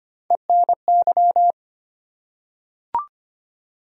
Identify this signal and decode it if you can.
Morse code, heard as a keyed tone, about 25 wpm, 700 Hz. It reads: ENY
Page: morse-code-ninja-practice-25wpm